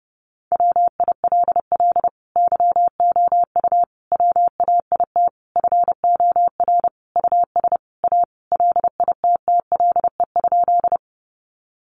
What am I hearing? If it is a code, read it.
Morse code, heard as a keyed tone, about 28 wpm, 700 Hz. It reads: WILL YOU WAIT FOR US A LITTLE?
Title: Sentences from Top 900 Words 30wpm